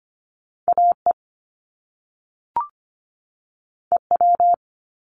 Morse code, heard as a keyed tone, about 25 wpm, 700 Hz. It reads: AE EW
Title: Top Two Letter Combinations in Words 25wpm